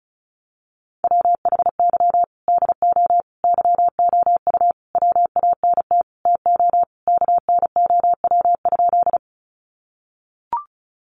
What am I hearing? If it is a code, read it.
Morse code, heard as a keyed tone, about 34 wpm, 700 Hz. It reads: WHY DO YOU WANT TO KNOW?
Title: Sentences from Top 200 Words 35wpm